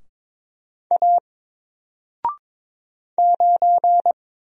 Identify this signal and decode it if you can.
Morse code, heard as a keyed tone, about 22 wpm, 700 Hz. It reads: A 9
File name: Single-Letter-Number-22wpm